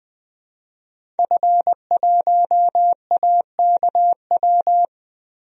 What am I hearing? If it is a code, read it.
Morse code, heard as a keyed tone, about 21 wpm, 700 Hz. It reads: F1AKW
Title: DXCC-100 Call Signs - 5 Characters - 20wpm